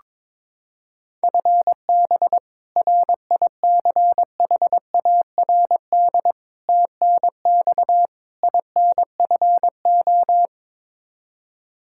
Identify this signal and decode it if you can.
Morse code, heard as a keyed tone, about 22 wpm, 700 Hz. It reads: FB RICHARD TNX INFO
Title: Practice QSO - Exchange 3 - More Info - 22wpm